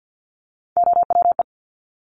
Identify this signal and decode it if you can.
Morse code, heard as a keyed tone, about 50 wpm, 700 Hz. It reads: OWE
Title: Top Three Letter Combinations in Words 50wpm